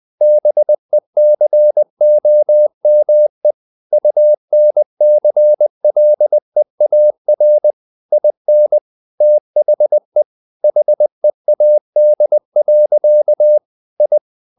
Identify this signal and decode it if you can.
Morse code, heard as a keyed tone, about 20 wpm, 600 Hz. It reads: BECOME UNCLEAR IN THE HEAD. I
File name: the old man and the sea part 2 20wpm